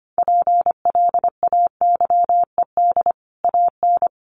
Morse code, heard as a keyed tone, about 25 wpm, 700 Hz. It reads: PLAYED AN
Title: Sets of 5 Words Encore2 25wpm